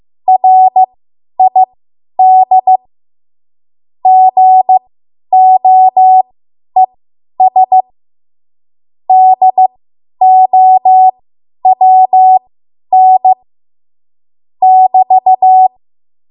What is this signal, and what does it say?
Morse code, heard as a keyed tone, about 15 wpm, spaced as for 10 wpm, 750 Hz. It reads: RID GOES DOWN <BT>
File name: Archive / 10 WPM / 10WPM